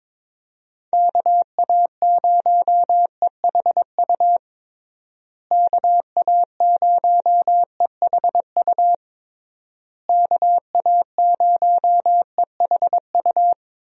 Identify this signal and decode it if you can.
Morse code, heard as a keyed tone, about 22 wpm, 700 Hz. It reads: KA0EHU KA0EHU KA0EHU
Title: USA Call Signs - 6 Characters - Repeated 3x - 22wpm